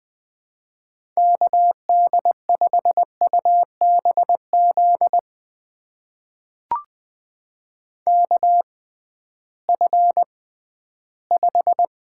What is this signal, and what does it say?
Morse code, heard as a keyed tone, about 20 wpm, spaced as for 13 wpm, 700 Hz. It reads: KD5UBZ KF5
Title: USA Call Signs - 6 Characters - 6x Character Spacing - 20wpm